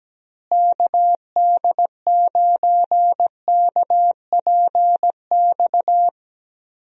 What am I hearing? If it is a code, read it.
Morse code, heard as a keyed tone, about 17 wpm, 700 Hz. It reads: KD9KPX
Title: USA Call Signs - 6 Characters - Repeated 3x - 17wpm